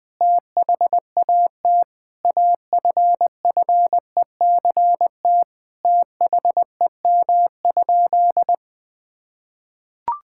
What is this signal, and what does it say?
Morse code, heard as a keyed tone, about 20 wpm, 700 Hz. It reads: THAT AFFECT THEM?
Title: Sentences from Top 1000 Words - Review Part 6 - 20wpm